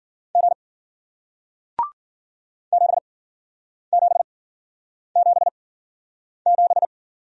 Morse code, heard as a keyed tone, about 58 wpm, spaced as for 9 wpm, 700 Hz. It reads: G 7777